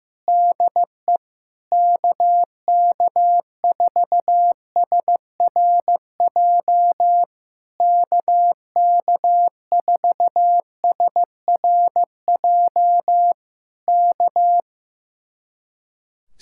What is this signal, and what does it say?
Morse code, heard as a keyed tone, about 15 wpm, 700 Hz. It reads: DE KK4SRJ KK4SRJ K
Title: Speed-Racing Common QSO Phrases Encore - 15wpm